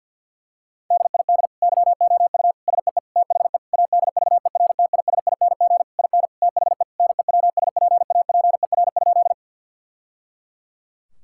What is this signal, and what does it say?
Morse code, heard as a keyed tone, about 50 wpm, 700 Hz. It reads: DID YOU SEE THE ADVERTISING IN THE NEWSPAPER?
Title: Sentences from Top 1000 Words - Review Part 6 - 50wpm